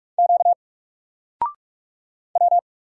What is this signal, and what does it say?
Morse code, heard as a keyed tone, about 44 wpm, 700 Hz. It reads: Q W
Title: Single Letter-Number Alt 45wpm